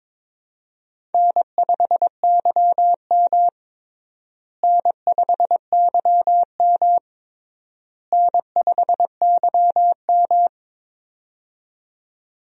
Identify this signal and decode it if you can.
Morse code, heard as a keyed tone, about 22 wpm, 700 Hz. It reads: N5YM N5YM N5YM
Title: USA Call Signs - 4 Characters - Repeated 3x - 22wpm